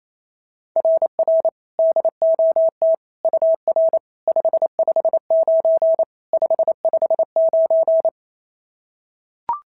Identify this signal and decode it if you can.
Morse code, heard as a keyed tone, about 28 wpm, 690 Hz. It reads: RR DOT UR 559 559
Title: SOTA QSO Practice 28wpm